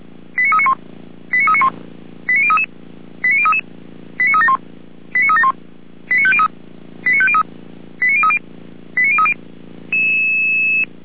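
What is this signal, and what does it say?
Five-tone selective call (ZVEI), heard as quick runs of five tones, tones of about 70 ms. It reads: 89391 89391 89033 89033 89371 89371 89503 89503 89030 89030
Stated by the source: Received by SDR.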